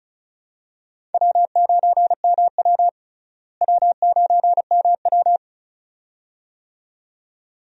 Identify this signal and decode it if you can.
Morse code, heard as a keyed tone, about 36 wpm, 700 Hz. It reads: W9MW W9MW
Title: USA Call Signs - 4 Characters - Repeated 2x - 35wpm